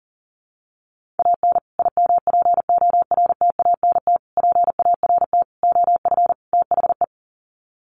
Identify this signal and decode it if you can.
Morse code, heard as a keyed tone, about 40 wpm, 700 Hz. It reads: AN IMPORTANT PART OF THE
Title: Sets of 5 Words Encore 40wpm